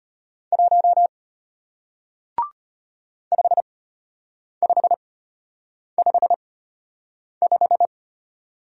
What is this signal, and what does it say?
Morse code, heard as a keyed tone, about 32 wpm, spaced as for 8 wpm, 700 Hz. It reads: 1 5555